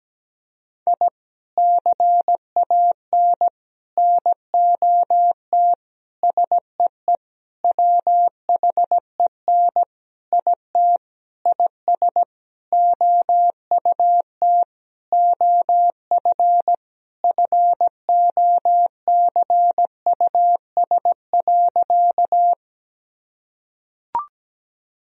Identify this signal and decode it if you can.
Morse code, heard as a keyed tone, about 17 wpm, 700 Hz. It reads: I CAN NOT SEE WHEN IT IS OUT OF FOCUS.
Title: Sentences from Top 1000 Words - Review Part 4 - 17wpm